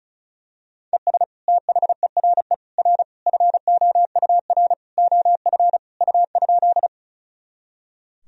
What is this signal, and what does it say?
Morse code, heard as a keyed tone, about 36 wpm, 700 Hz. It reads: ES THERE R FOUR OF U?
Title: Sentences from Top 300 Words and Abbreviations 35wpm